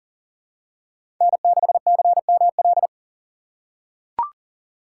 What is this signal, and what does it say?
Morse code, heard as a keyed tone, about 40 wpm, 700 Hz. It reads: N6CML
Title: USA Call Signs - 4x Character Spacing - 40wpm